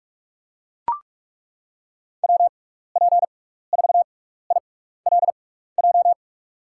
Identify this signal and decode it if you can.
Morse code, heard as a keyed tone, about 41 wpm, spaced as for 15 wpm, 700 Hz. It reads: WP4ILJ
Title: USA Call Signs - 6x Character Spacing - 45wpm